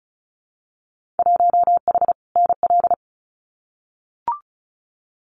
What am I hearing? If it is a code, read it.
Morse code, heard as a keyed tone, about 36 wpm, 700 Hz. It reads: 1H NL
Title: Winter Field Day Exchanges 35wpm